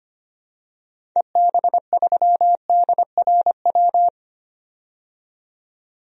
Morse code, heard as a keyed tone, about 25 wpm, 700 Hz. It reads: EB3DRW
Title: DXCC-100 Call Signs - 6 Characters - 25wpm